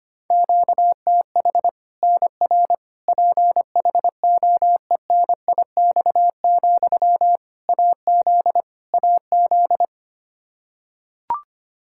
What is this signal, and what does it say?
Morse code, heard as a keyed tone, about 25 wpm, 700 Hz. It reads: QTH NR PHOENIX, AZ AZ
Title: K1USN SST QSO Practice 25wpm